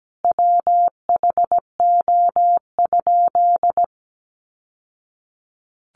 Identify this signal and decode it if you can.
Morse code, heard as a keyed tone, about 17 wpm, 700 Hz. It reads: WHO?
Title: Sentences from Top 800 Words 17wpm